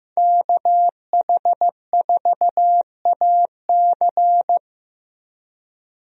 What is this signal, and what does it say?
Morse code, heard as a keyed tone, about 15 wpm, 700 Hz. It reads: KH4AC
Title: USA Call Signs - 5 Characters - 15wpm